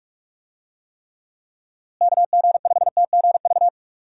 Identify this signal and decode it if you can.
Morse code, heard as a keyed tone, about 44 wpm, 700 Hz. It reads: KG5TGV
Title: USA Call Signs - 6 Characters - Repeated 3x - 45wpm